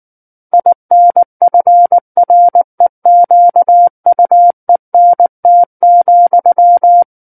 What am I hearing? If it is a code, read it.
Morse code, heard as a keyed tone, about 19 wpm, 700 Hz. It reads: INFREQUENT,
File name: Book One / 10-Chapter 10 - 19 WPM